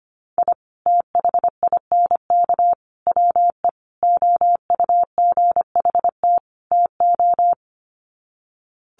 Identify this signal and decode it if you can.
Morse code, heard as a keyed tone, about 25 wpm, 700 Hz. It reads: I THINK WE OUGHT TO
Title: Sets of 5 Words Encore 25wpm